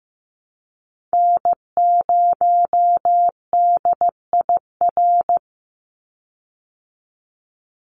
Morse code, heard as a keyed tone, about 15 wpm, 700 Hz. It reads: N0DIR
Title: Call Signs Repeated 2x - 15wpm